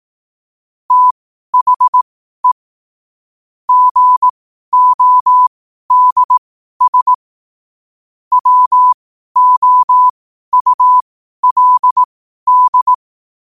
Morse code, heard as a keyed tone, about 18 wpm, spaced as for 12 wpm, 1000 Hz. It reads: THE GODS WOULD
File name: qotdfiles / 12wpm